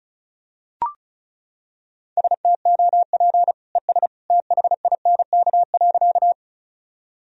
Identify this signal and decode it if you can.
Morse code, heard as a keyed tone, about 36 wpm, 700 Hz. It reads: STOP ES THINK.
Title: Sentences from Top 300 Words and Abbreviations 35wpm